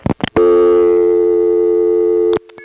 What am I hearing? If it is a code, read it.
Dial tone (North America), 350 + 440 Hz.
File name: phonetrips / CoinControlHQ